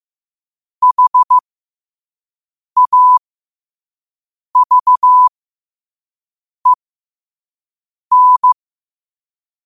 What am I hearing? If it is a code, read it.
Morse code, heard as a keyed tone, about 15 wpm, spaced as for 5 wpm, 1000 Hz. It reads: HAVEN
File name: qotdfiles / 5wpm